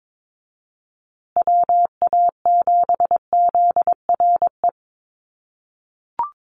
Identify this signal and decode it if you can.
Morse code, heard as a keyed tone, about 22 wpm, 700 Hz. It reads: WA7ZRE
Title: USA Call Signs Repeated 2x 22wpm